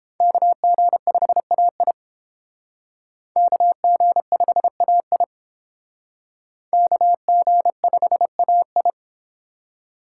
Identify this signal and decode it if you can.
Morse code, heard as a keyed tone, about 31 wpm, 700 Hz. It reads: KG5AI KG5AI KG5AI